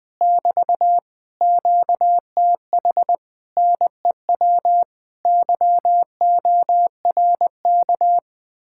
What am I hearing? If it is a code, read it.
Morse code, heard as a keyed tone, about 20 wpm, 700 Hz. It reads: <BT> QTH NEW YORK
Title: Speed-Racing Common QSO Phrases Encore - 20wpm